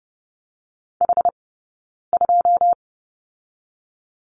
Morse code, heard as a keyed tone, about 30 wpm, 700 Hz. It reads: H 2
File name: Lesson 16d - Letter c - Two Characters 30wpm